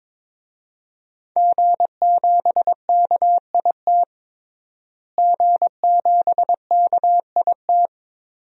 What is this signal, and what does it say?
Morse code, heard as a keyed tone, about 22 wpm, 700 Hz. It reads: G7KIT G7KIT
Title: DXCC-100 Call Signs - 5 Characters - Repeated 2x - 22wpm